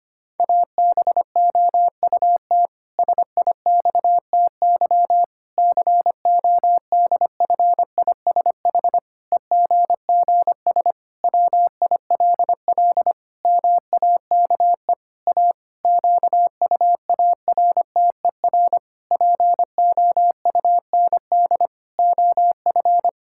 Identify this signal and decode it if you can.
Morse code, heard as a keyed tone, about 25 wpm, 700 Hz. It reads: ABOUT SIXTY CODFISH EGGS WILL MAKE A QUARTER POUND OF
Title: Pangram Challenge v2 - 25wpm